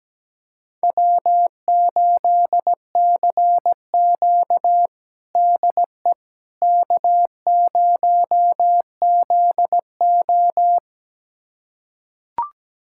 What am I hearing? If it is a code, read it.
Morse code, heard as a keyed tone, about 17 wpm, 700 Hz. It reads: W8CQ DE K0ZO
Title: USA Callsign DE Callsign - 4 Characters - 17wpm